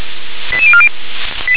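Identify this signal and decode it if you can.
Five-tone selective call (ZVEI), heard as quick runs of five tones, tones of about 70 ms. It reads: 88040
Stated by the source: ZVEI five-tone selective call 88119, received by SDR.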